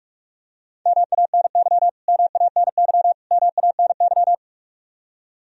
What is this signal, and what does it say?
Morse code, heard as a keyed tone, about 46 wpm, 700 Hz. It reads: MANY MANY MANY
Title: Top 100 Words Repeated 45wpm